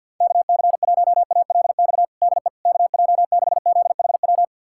Morse code, heard as a keyed tone, about 52 wpm, 700 Hz. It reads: KC1ALX DE KJ6ZHW
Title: USA Callsign DE Callsign - 6 Characters - 50wpm